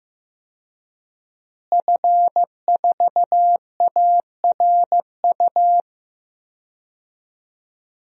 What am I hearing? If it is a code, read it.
Morse code, heard as a keyed tone, about 15 wpm, 700 Hz. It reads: F4ARU
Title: DXCC-100 Call Signs - 5 Characters - Repeated 3x - 15wpm